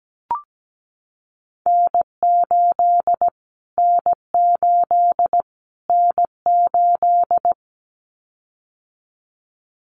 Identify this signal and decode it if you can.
Morse code, heard as a keyed tone, about 17 wpm, 700 Hz. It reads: N8 N8 N8
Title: Callsign Prefixes for Top 25 DXCC - Repeated 3x - 17wpm